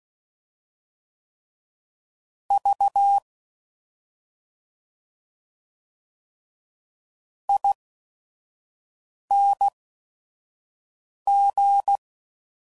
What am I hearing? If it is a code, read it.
Morse code, heard as a keyed tone, about 16 wpm, spaced as for 5 wpm, 800 Hz. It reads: V ING
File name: Suffixes / Special ING's / Sngl ING's